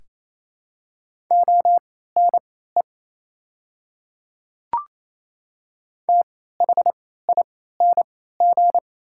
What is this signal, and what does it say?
Morse code, heard as a keyed tone, about 28 wpm, spaced as for 16 wpm, 700 Hz. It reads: ONE THING
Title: 3x Character Spacing - Top 100 Words - 28wpm